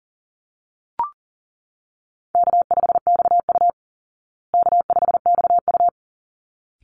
Morse code, heard as a keyed tone, about 40 wpm, 700 Hz. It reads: K5XU K5XU
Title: Easy Call Signs Repeated 2x - 40wpm